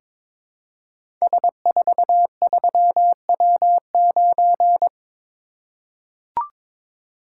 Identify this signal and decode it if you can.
Morse code, heard as a keyed tone, about 21 wpm, 700 Hz. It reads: S43W9
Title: Code Groups - 5 Letter:Numbers - 22wpm